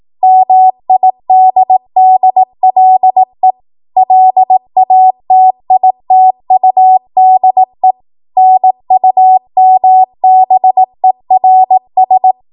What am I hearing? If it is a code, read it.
Morse code, heard as a keyed tone, about 18 wpm, 750 Hz. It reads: MIDDLE LATITUDE NUMBERS
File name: Archive / 18 WPM / 18WPM